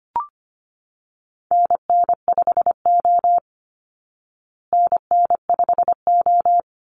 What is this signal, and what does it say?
Morse code, heard as a keyed tone, about 25 wpm, 700 Hz. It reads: NN5O NN5O
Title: Call Signs Repeated 2x - 25wpm